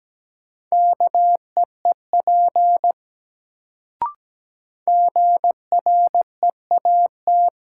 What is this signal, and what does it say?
Morse code, heard as a keyed tone, about 17 wpm, 700 Hz. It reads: KEEP GREAT
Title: Sets of 2 Words - New Word Drill - 17wpm